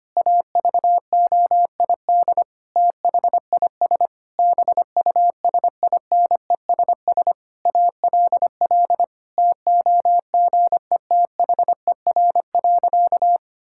Morse code, heard as a keyed tone, about 25 wpm, 700 Hz. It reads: AVOID THIS BUSINESS ALL TOGETHER.
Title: Sentences from Top 1000 Words - Review Part 6 - 25wpm